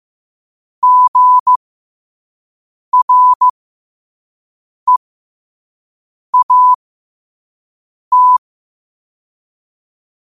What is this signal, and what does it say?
Morse code, heard as a keyed tone, about 15 wpm, spaced as for 5 wpm, 1000 Hz. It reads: GREAT
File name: qotdfiles / 5wpm